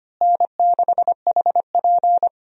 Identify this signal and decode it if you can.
Morse code, heard as a keyed tone, about 25 wpm, 700 Hz. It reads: N6HP
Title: USA Call Signs - 4 Characters - Repeated 2x - 25wpm